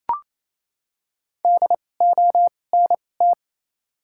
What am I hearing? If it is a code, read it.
Morse code, heard as a keyed tone, about 28 wpm, spaced as for 20 wpm, 690 Hz. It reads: DONT
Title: 2x Character Spacing - Top 500 Words - 28wpm